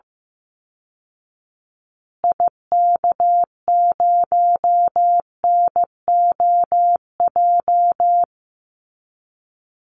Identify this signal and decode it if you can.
Morse code, heard as a keyed tone, about 15 wpm, 700 Hz. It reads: IK0NOJ
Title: Call Signs Repeated 2x - 15wpm